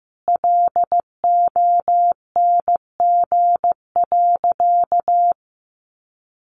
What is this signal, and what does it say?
Morse code, heard as a keyed tone, about 15 wpm, 700 Hz. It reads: LONG.
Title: Sentences from Top 1100 Words 15wpm